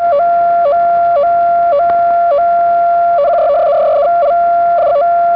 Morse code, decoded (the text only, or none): none